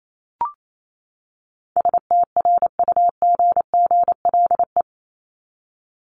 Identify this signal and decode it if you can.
Morse code, heard as a keyed tone, about 28 wpm, 700 Hz. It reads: STRUGGLE
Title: Top 1751-2000 Words 28wpm